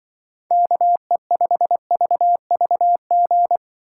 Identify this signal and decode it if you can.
Morse code, heard as a keyed tone, about 25 wpm, 700 Hz. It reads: KE5VVG
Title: Speed Racing - USA 2x3 Call Signs 20wpm